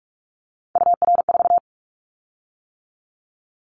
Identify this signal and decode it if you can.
Morse code, heard as a keyed tone, about 46 wpm, 700 Hz. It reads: UR4
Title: Callsign Prefixes for Top 25 DXCC - 45wpm